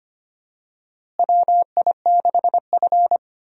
Morse code, heard as a keyed tone, about 25 wpm, 700 Hz. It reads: WI6F
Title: USA Call Signs - 4 Characters - Repeated 2x - 25wpm